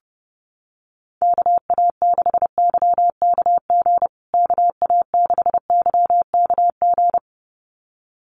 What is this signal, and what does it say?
Morse code, heard as a keyed tone, about 30 wpm, 700 Hz. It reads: KA6YKG KA6YKG
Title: USA Call Signs Repeated 2x 30wpm